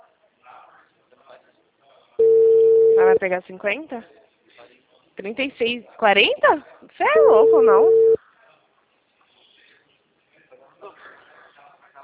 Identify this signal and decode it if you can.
Ringback tone (China, 450 Hz), with speech.